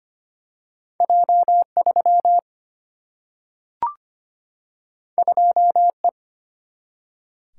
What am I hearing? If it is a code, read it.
Morse code, heard as a keyed tone, about 25 wpm, 700 Hz. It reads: J3 2E
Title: Single-Letter Single-Number 25wpm